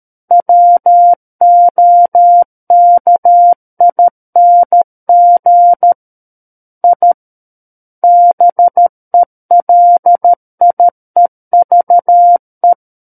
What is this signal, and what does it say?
Morse code, heard as a keyed tone, about 13 wpm, 700 Hz. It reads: WOKING I BELIEVE
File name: Book One / 04-Chapter 4 - 13 WPM